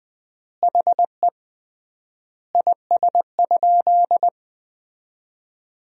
Morse code, heard as a keyed tone, about 20 wpm, 700 Hz. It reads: HE IS?
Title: 3x Word-Spacing - Sentences from Top 300 Words - 20wpm